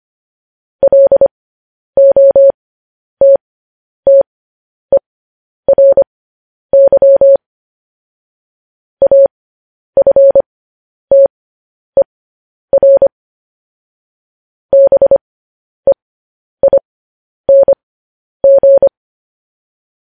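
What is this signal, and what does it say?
Morse code, heard as a keyed tone, about 25 wpm, spaced as for 10 wpm, 550 Hz. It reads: LOTTERY AFTER BEING
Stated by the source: Morse transmission of news headlines.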